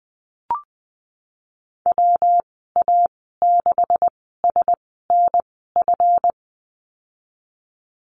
Morse code, heard as a keyed tone, about 20 wpm, spaced as for 14 wpm, 700 Hz. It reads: WA6SNF
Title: 140 Lesson 23g - Letter f - Call Signs 20wpm